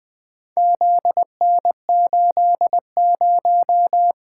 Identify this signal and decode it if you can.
Morse code, heard as a keyed tone, about 20 wpm, 700 Hz. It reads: ZN80
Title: Four Letter-Number Combinations - 20wpm